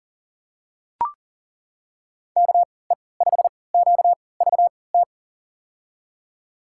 Morse code, heard as a keyed tone, about 39 wpm, spaced as for 22 wpm, 700 Hz. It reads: KE5QVT
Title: USA Call Signs - 6 Characters - 3x Character Spacing - 40wpm